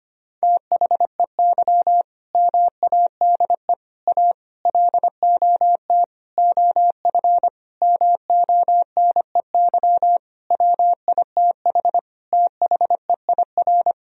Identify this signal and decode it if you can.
Morse code, heard as a keyed tone, about 25 wpm, 700 Hz. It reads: THEY MADE A LOT OF MONEY WITH THEIR
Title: Sentences from Top 1000 Words - Review Part 4 - 25wpm